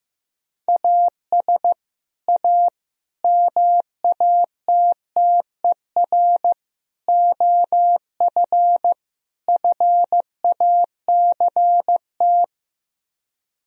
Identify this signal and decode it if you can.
Morse code, heard as a keyed tone, about 15 wpm, 700 Hz. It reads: AS A MATTER OF FACT
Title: Speed-Racing - Sets of 5 Words 15wpm